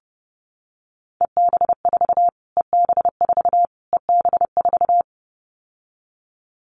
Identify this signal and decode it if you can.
Morse code, heard as a keyed tone, about 30 wpm, 690 Hz. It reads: EB4 EB4 EB4
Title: Callsign Prefixes for Top 25 DXCC - Repeated 3x - 30wpm